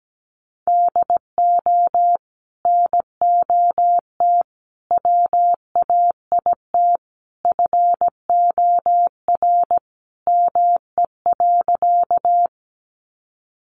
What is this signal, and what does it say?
Morse code, heard as a keyed tone, about 17 wpm, 700 Hz. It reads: DO NOT WAIT FOR ME.
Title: Sentences from Top 600 Words 17wpm